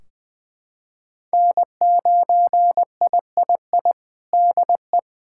Morse code, heard as a keyed tone, about 21 wpm, 700 Hz. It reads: N9III DE
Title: Practice QSO - Exchange 4 - Ending - 20wpm